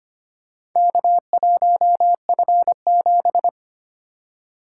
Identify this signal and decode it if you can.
Morse code, heard as a keyed tone, about 25 wpm, 700 Hz. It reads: K1F7
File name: practice / 25wpm / Four Letter-Number Combinations - 25wpm